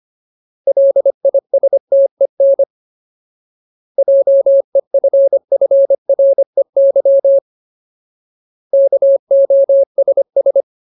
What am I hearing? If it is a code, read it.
Morse code, heard as a keyed tone, about 25 wpm, 550 Hz. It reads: LISTEN JEFFREY KOSS
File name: Ultimate Word List / 25WPM Ultimate